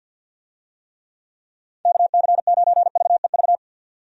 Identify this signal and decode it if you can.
Morse code, heard as a keyed tone, about 51 wpm, 700 Hz. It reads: KC9VEV
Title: USA Call Signs - 6 Characters - 50wpm